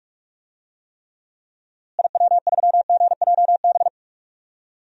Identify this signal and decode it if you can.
Morse code, heard as a keyed tone, about 46 wpm, 700 Hz. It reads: IW3GJB
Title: DXCC-100 Call Signs - 6 Characters - Repeated 2x - 45wpm